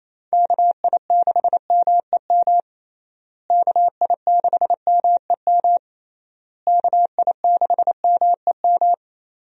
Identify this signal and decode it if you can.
Morse code, heard as a keyed tone, about 28 wpm, 700 Hz. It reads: KI6MEM KI6MEM KI6MEM
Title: USA Call Signs - 6 Characters - Repeated 3x - 28wpm